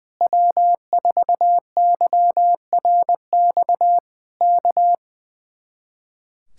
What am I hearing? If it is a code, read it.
Morse code, heard as a keyed tone, about 20 wpm, 700 Hz. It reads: W4YRX K
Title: Speed-Racing Common QSO Phrases Encore - 20wpm